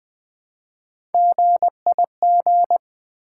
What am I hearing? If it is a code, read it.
Morse code, heard as a keyed tone, about 20 wpm, 700 Hz. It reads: GIG
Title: Build A Word - 1 Letter At A Time - 3-Letter Words - 20wpm